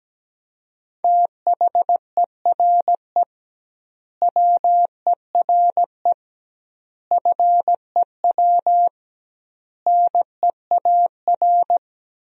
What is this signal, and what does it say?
Morse code, heard as a keyed tone, about 17 wpm, 700 Hz. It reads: THERE WERE FEW NEAR
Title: 2x Word-Spacing - Sentences from Top 300 Words - 17wpm